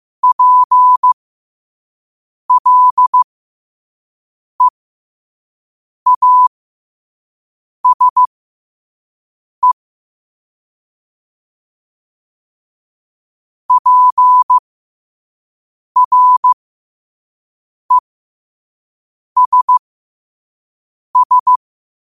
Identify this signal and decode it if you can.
Morse code, heard as a keyed tone, about 15 wpm, spaced as for 5 wpm, 1000 Hz. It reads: PLEASE PRESS
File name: qotdfiles / 5wpm